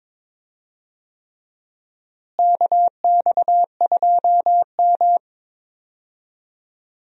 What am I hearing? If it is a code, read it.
Morse code, heard as a keyed tone, about 22 wpm, 700 Hz. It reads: KX2M
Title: USA Call Signs - 4 Characters - Repeated 3x - 22wpm